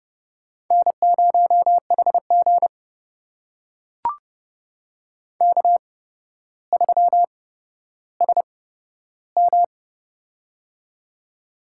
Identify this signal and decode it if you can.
Morse code, heard as a keyed tone, about 30 wpm, spaced as for 13 wpm, 700 Hz. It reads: N0HG K3SM
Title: USA Call Signs - 4 Characters - 8x Character Spacing - 30wpm